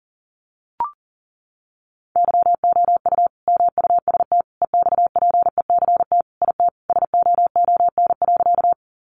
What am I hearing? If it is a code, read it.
Morse code, heard as a keyed tone, about 40 wpm, 700 Hz. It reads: YOU MUST EXPECT IT SOON.
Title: Sentences from Top 1100 Words 40wpm